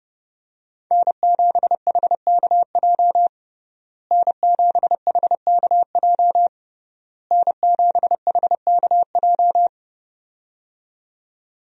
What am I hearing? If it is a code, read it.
Morse code, heard as a keyed tone, about 30 wpm, 700 Hz. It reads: N7HKJ N7HKJ N7HKJ
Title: USA Call Signs - 5 Characters - Repeated 3x - 30wpm